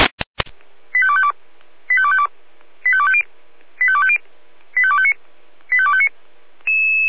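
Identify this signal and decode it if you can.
Five-tone selective call (ZVEI), heard as quick runs of five tones, tones of about 70 ms. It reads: 85252 85252 85250 85250 85259 85259
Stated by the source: SDR-received ZVEI five-tone selective call, code 85220, recorded by feuerwehr-orlamuende fire brigade.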